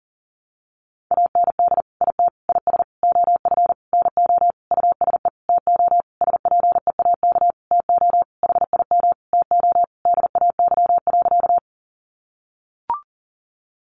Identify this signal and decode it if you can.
Morse code, heard as a keyed tone, about 40 wpm, 700 Hz. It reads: AND IT IS OF NO USE TO SPEAK TO HIM TO DAY.
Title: Sentences from Top 1100 Words 40wpm